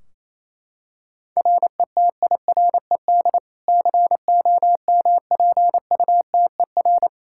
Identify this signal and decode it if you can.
Morse code, heard as a keyed tone, about 28 wpm, 700 Hz. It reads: RETIRED COMPUTER
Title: Practice QSO - Exchange 3 - More Info - 28wpm